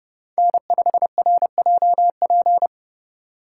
Morse code, heard as a keyed tone, about 28 wpm, 700 Hz. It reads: N5RJP